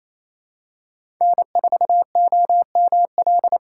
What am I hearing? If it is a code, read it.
Morse code, heard as a keyed tone, about 28 wpm, 700 Hz. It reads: N4OML
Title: USA Call Signs - 4x Character Spacing - 28wpm